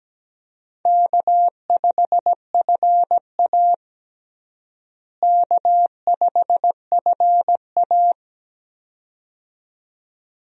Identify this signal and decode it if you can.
Morse code, heard as a keyed tone, about 17 wpm, 700 Hz. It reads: K5FA K5FA
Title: USA Call Signs - 4 Characters - Repeated 2x - 17wpm